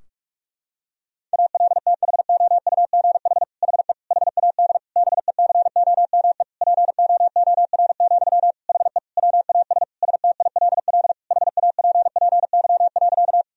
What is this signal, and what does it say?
Morse code, heard as a keyed tone, about 46 wpm, 700 Hz. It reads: ALTHOUGH HE HAD BECOME POOR, HE WAS STILL HAPPY.
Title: Sentences from Top 1000 Words - Review Part 3 - 45wpm